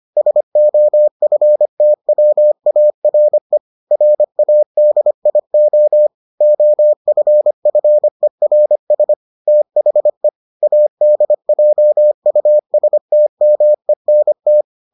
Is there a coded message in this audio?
Yes: Morse code from 0.2 to 14.6 s, heard as a keyed tone, about 25 wpm, 600 Hz. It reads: SOFTWARE RADIO OFFERS THE ADJUSTMENT